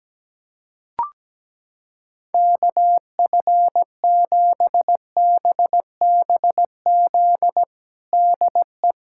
Morse code, heard as a keyed tone, about 17 wpm, 700 Hz. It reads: KF7BBZ DE
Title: USA Callsign DE Callsign - 6 Characters - 17wpm